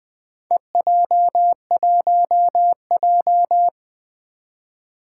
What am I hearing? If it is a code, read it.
Morse code, heard as a keyed tone, about 20 wpm, 700 Hz. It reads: EJ1J
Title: Four Challenging Letter-Number Combinations 20wpm